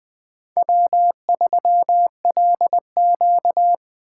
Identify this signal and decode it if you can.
Morse code, heard as a keyed tone, about 21 wpm, 700 Hz. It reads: W3LQ